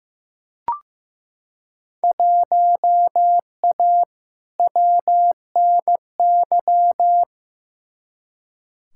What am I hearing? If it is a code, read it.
Morse code, heard as a keyed tone, about 15 wpm, 700 Hz. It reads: 1A WNY
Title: Field Day Exchanges - 15wpm